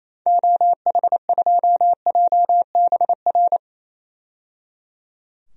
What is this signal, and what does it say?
Morse code, heard as a keyed tone, about 27 wpm, 700 Hz. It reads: OH2JBR
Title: DXCC-100 Call Signs - 6 Characters - 28wpm